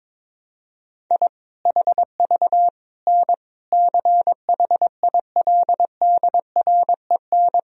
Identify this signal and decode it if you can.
Morse code, heard as a keyed tone, about 21 wpm, 700 Hz. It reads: I HV N CHILDREN
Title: Sentences from Top 400 Words and Abbreviations 22wpm